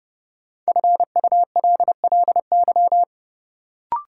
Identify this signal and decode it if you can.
Morse code, heard as a keyed tone, about 30 wpm, 700 Hz. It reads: FULLY
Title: Sentences from Top 1300 Words - New Word Drill - 30wpm